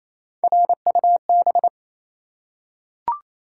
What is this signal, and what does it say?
Morse code, heard as a keyed tone, about 27 wpm, 700 Hz. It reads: RUB
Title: Three Letter Words - 28wpm